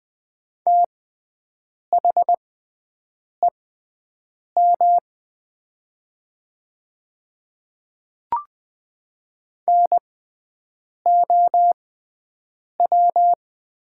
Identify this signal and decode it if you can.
Morse code, heard as a keyed tone, about 20 wpm, spaced as for 7 wpm, 700 Hz. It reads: THEM NOW
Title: 6x Character Spacing - Top 100 Words - 20wpm